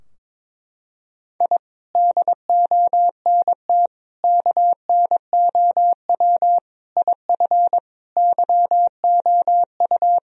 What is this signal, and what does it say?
Morse code, heard as a keyed tone, about 22 wpm, 700 Hz. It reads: I DONT KNOW IF YOU
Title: Sentences from Top 1000 Words - Review Part 2 - 22wpm